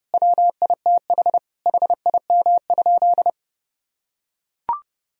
Morse code, heard as a keyed tone, about 30 wpm, 700 Hz. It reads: WITH HIM?
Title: Sentences from Top 1000 Words - Review Part 3 - 30wpm